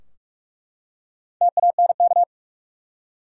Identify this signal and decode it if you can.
Morse code, heard as a keyed tone, about 41 wpm, 700 Hz. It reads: TANK
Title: Four Letter Words - 45wpm